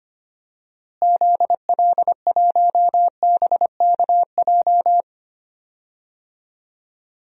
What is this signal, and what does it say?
Morse code, heard as a keyed tone, about 25 wpm, 700 Hz. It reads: ZL1BKJ
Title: DXCC-100 Call Signs - 6 Characters - Repeated 2x - 25wpm